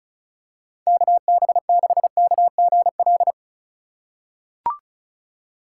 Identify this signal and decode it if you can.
Morse code, heard as a keyed tone, about 36 wpm, 700 Hz. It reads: KB6KGL
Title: USA Call Signs - 6 Characters - 2x Character Spacing - 35wpm